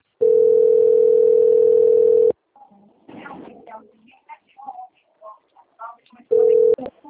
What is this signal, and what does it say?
Ringback tone (North America), 440 + 480 Hz.